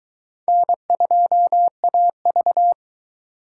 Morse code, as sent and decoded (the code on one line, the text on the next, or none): -. ..--- .- ...-
N2AV